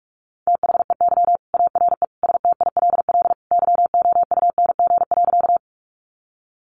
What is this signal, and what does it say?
Morse code, heard as a keyed tone, about 44 wpm, 700 Hz. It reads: THEY ARE STILL YOUNG.
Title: Sentences from Top 400 Words 45wpm